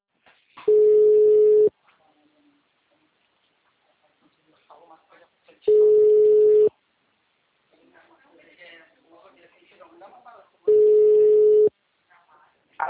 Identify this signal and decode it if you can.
Ringback tone (Europe, 425 Hz), with speech.